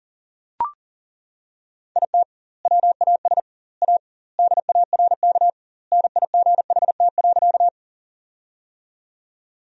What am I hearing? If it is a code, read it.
Morse code, heard as a keyed tone, about 40 wpm, 700 Hz. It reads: IT WAS A DARK NIGHT.
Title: 2x Word-Spacing - Sentences from Top 500 Words - 40wpm